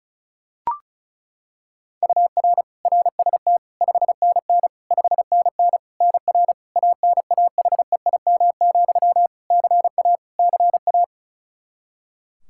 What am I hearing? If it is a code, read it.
Morse code, heard as a keyed tone, about 34 wpm, 700 Hz. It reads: UR RST 5NN 5NN NR ANAHEIM, CA CA
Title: Practice QSO - Exchange 1 - Beginning - 35wpm